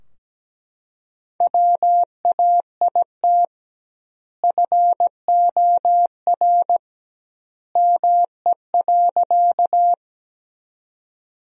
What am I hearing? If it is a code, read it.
Morse code, heard as a keyed tone, about 17 wpm, 700 Hz. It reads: WAIT FOR ME.
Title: 2x Word-Spacing - Sentences from Top 500 Words - 17wpm